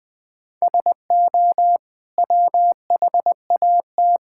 Morse code, heard as a keyed tone, about 20 wpm, 700 Hz. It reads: SO WHAT
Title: Sentences with 8-Word Phrases - 20wpm